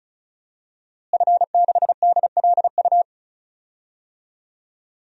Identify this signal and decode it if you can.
Morse code, heard as a keyed tone, about 36 wpm, 700 Hz. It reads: F6DLU
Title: DXCC-100 Call Signs - 5 Characters - Repeated 2x - 35wpm